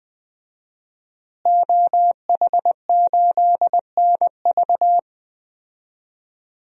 Morse code, heard as a keyed tone, about 20 wpm, 700 Hz. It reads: OH8NV
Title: DXCC-100 Call Signs - 5 Characters - Repeated 3x - 20wpm